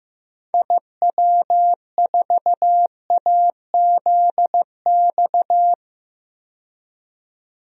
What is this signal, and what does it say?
Morse code, heard as a keyed tone, about 15 wpm, 700 Hz. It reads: IW4AZX
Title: DXCC-100 Call Signs - 6 Characters - 15wpm